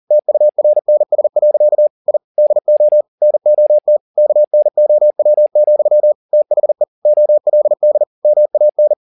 Morse code, heard as a keyed tone, about 40 wpm, 600 Hz. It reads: TURNS. I DO NOT KNOW, THE OLD MAN
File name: the old man and the sea 2 40wpm